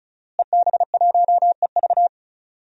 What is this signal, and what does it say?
Morse code, heard as a keyed tone, about 36 wpm, 700 Hz. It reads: EB1EV
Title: DXCC-100 Call Signs - 5 Characters - Repeated 2x - 35wpm